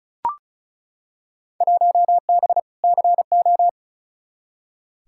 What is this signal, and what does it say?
Morse code, heard as a keyed tone, about 36 wpm, 700 Hz. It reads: 1B CO
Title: Field Day Exchanges - 35wpm